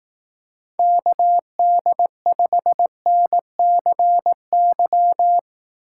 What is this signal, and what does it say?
Morse code, heard as a keyed tone, about 18 wpm, 700 Hz. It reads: KD5NCY
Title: Speed Racing - USA Call Signs 15wpm